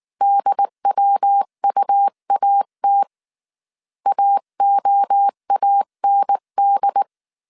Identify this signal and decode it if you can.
Morse code, heard as a keyed tone, about 19 wpm, 800 Hz. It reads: DWUAT AOAND
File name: CW / DET / 1 3 1500C S B 18wpm 1916